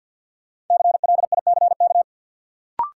Morse code, heard as a keyed tone, about 52 wpm, 700 Hz. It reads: CLICK
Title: Sentences from Top 1100 Words - New Word Drill Alt - 50wpm